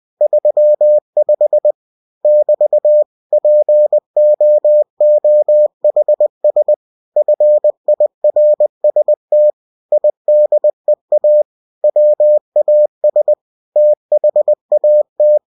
Morse code, heard as a keyed tone, about 20 wpm, 600 Hz. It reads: 35 <BT> POOHS FIRST IDEA WAS THAT
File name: winnie-20wpm